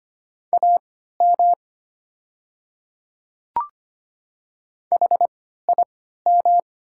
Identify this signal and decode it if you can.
Morse code, heard as a keyed tone, about 25 wpm, spaced as for 14 wpm, 700 Hz. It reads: AM HIM
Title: 3x Character Spacing - Top 500 Words - 25wpm